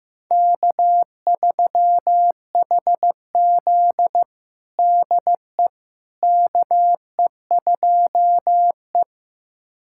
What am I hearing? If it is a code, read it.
Morse code, heard as a keyed tone, about 15 wpm, 700 Hz. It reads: K3HZ DE KE2E
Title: USA Callsign DE Callsign - 4 Characters - 15wpm